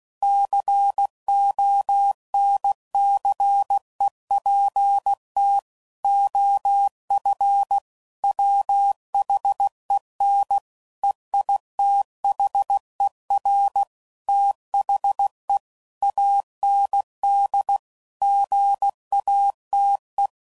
Morse code, heard as a keyed tone, about 16 wpm, 800 Hz. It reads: CONCEPT OF WHEN EITHER THE AND GATE
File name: CW Code Groups @15WPM / Technical Subjects / AND Gate